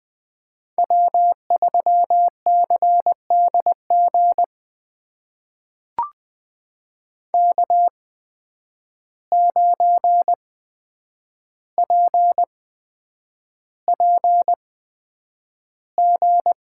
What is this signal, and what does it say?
Morse code, heard as a keyed tone, about 20 wpm, spaced as for 9 wpm, 700 Hz. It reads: W3CDG K9PPG
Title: USA Call Signs - 5 Characters - 8x Character Spacing - 20wpm